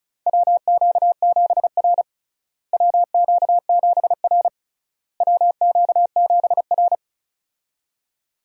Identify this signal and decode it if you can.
Morse code, heard as a keyed tone, about 36 wpm, 700 Hz. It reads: WQ7R WQ7R WQ7R
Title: USA Call Signs - 4 Characters - Repeated 3x - 35wpm